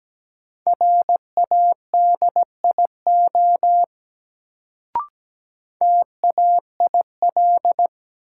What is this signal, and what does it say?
Morse code, heard as a keyed tone, about 17 wpm, 700 Hz. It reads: RADIO TAIL
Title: Sentences from Top 1000 Words - Review Part 3 - New Word Drill - 17wpm